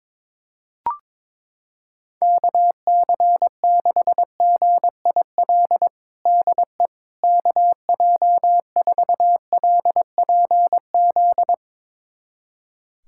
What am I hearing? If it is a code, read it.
Morse code, heard as a keyed tone, about 22 wpm, 700 Hz. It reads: KC6GIL DE KJ4LPZ
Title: USA Callsign DE Callsign - 6 Characters - 22wpm